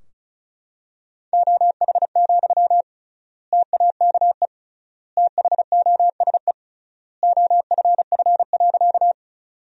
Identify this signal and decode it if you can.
Morse code, heard as a keyed tone, about 36 wpm, 700 Hz. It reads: OH, TAKE THOSE OFF.
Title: 3x Word-Spacing - Sentences from Top 500 Words - 35wpm